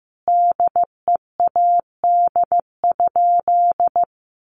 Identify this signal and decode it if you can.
Morse code, heard as a keyed tone, about 15 wpm, 700 Hz. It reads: DEAD?
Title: Sentences from Top 1500 Words 15wpm